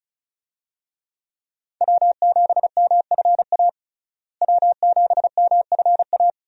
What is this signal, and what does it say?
Morse code, heard as a keyed tone, about 36 wpm, 700 Hz. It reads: W7MFA W7MFA
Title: USA Call Signs - 5 Characters - Repeated 2x - 35wpm